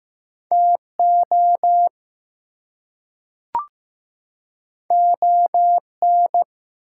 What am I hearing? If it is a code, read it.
Morse code, heard as a keyed tone, about 15 wpm, 700 Hz. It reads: TO ON
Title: Two Letter Words - 15wpm